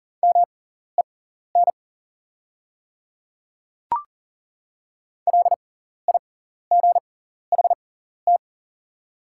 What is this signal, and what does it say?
Morse code, heard as a keyed tone, about 40 wpm, spaced as for 14 wpm, 690 Hz. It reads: MEN LIGHT
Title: 6x Character Spacing - Top 300 Words - 40wpm